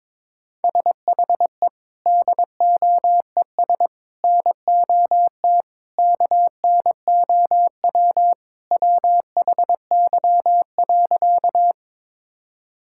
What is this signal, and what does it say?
Morse code, heard as a keyed tone, about 22 wpm, 700 Hz. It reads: SHE DOES NOT KNOW WHY.
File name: practice / 22wpm / Sentences with 4-Word Phrases - 22wpm